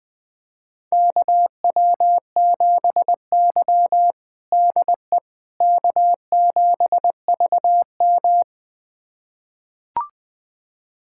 Morse code, heard as a keyed tone, about 21 wpm, 700 Hz. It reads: KW7Y DE K7VM
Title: USA Callsign DE Callsign - 4 Characters - 20wpm